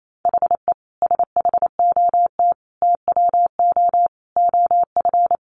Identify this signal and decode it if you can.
Morse code, heard as a keyed tone, about 28 wpm, 700 Hz. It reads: HE SHOT TWO OF
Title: Sentences from Top 1100 Words 28wpm